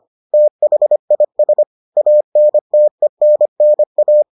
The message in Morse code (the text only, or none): THIS ANTENNA